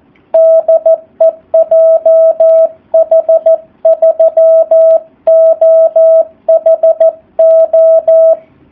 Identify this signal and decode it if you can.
Morse code, heard as a keyed tone, about 14 wpm, 650 Hz, with radio noise. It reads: DE JH3OHO
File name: jh8cbh / 2019cw / SP04- w